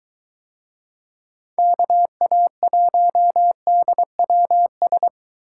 Morse code, heard as a keyed tone, about 23 wpm, 700 Hz. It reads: KA1DWS